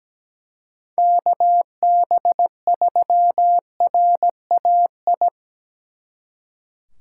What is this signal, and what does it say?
Morse code, heard as a keyed tone, about 19 wpm, 700 Hz. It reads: KB3RAI
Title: Speed Racing - USA Call Signs 17wpm